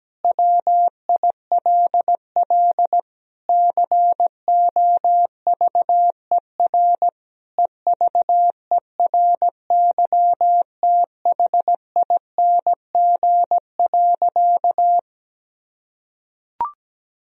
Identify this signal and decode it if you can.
Morse code, heard as a keyed tone, about 17 wpm, 700 Hz. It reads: WILL COVER EVERYTHING.
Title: Sentences from Top 1000 Words - Review Part 1 - 17wpm